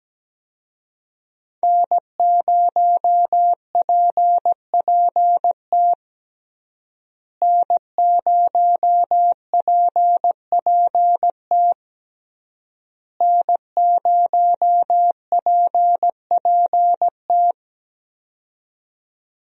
Morse code, heard as a keyed tone, about 17 wpm, 700 Hz. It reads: N0PPT N0PPT N0PPT
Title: USA Call Signs - 5 Characters - Repeated 3x - 17wpm